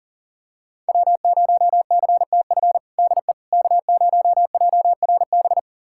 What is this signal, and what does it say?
Morse code, heard as a keyed tone, about 40 wpm, 700 Hz. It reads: W0CTF DE K0JRB
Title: USA Callsign DE Callsign - 5 Characters - 40wpm